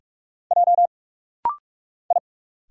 Morse code, heard as a keyed tone, about 46 wpm, 700 Hz. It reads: J I
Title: Single Letters - Rapid-Fire 45wpm